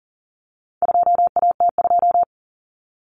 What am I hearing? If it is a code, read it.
Morse code, heard as a keyed tone, about 40 wpm, 700 Hz. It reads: 2AT2
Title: Common Cut Numbers - 4-Digit 40wpm